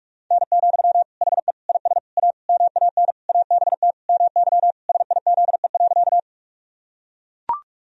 Morse code, heard as a keyed tone, about 41 wpm, 700 Hz. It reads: N, HE IS A MAN ABT MY SIZE.
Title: Sentences from Top 400 Words and Abbreviations 45wpm